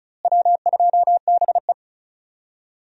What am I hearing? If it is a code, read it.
Morse code, heard as a keyed tone, about 34 wpm, 700 Hz. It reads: W2BE
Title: USA Call Signs - 4 Characters - Repeated 2x - 35wpm